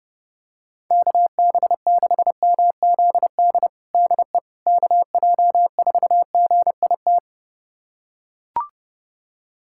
Morse code, heard as a keyed tone, about 32 wpm, 700 Hz. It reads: KB6MZD DE KJ4GIT
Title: USA Callsign DE Callsign - 6 Characters - 30wpm